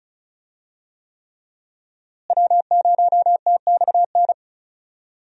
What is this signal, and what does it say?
Morse code, heard as a keyed tone, about 36 wpm, 700 Hz. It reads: W0TXN
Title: USA Call Signs - 5 Characters - Repeated 3x - 35wpm